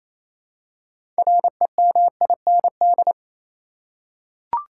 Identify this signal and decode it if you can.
Morse code, heard as a keyed tone, about 28 wpm, 700 Hz. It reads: REMIND
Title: Six Letter Words - 28wpm